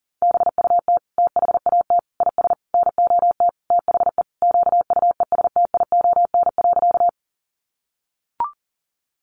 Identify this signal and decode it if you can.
Morse code, heard as a keyed tone, about 40 wpm, 700 Hz. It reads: BUT THAT IS NOT THE QUESTION.
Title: Sentences from Top 600 Words 40wpm